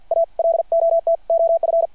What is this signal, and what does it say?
Morse code, heard as a keyed tone, about 52 wpm, 660 Hz. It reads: A LOT OF
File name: archived / CRSnet